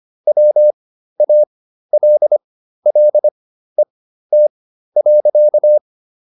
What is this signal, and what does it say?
Morse code, heard as a keyed tone, about 25 wpm, spaced as for 13 wpm, 600 Hz. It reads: WALLET.